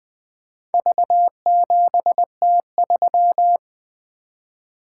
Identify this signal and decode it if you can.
Morse code, heard as a keyed tone, about 20 wpm, 700 Hz. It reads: V7T3
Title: Four Letter-Number Combinations - 20wpm